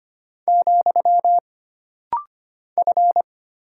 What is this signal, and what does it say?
Morse code, heard as a keyed tone, about 25 wpm, 700 Hz. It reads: , F
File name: Single Character (Letter:Number:Punctuation:Prosign) - Rapid-Fire - 25wpm